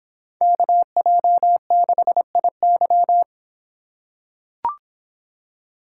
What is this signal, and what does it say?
Morse code, heard as a keyed tone, about 26 wpm, 700 Hz. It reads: KJ6IY